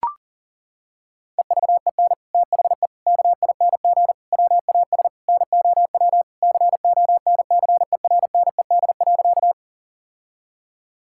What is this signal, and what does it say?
Morse code, heard as a keyed tone, about 37 wpm, 700 Hz. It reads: EVEN THE KING WAS NOW CONCERNED.
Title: Sentences from Top 1000 Words - Review Part 6 - 40wpm